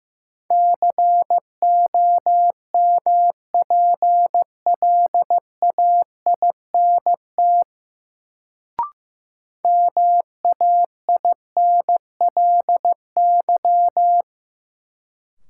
Morse code, heard as a keyed tone, about 16 wpm, 700 Hz. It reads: COMPLAINT MAINLY
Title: Sentences from Top 1400 Words - New Word Drill - 15wpm